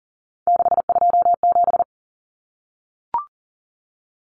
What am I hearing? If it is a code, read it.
Morse code, heard as a keyed tone, about 40 wpm, 700 Hz. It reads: 627
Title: Common Cut Numbers - 3-Digit 40wpm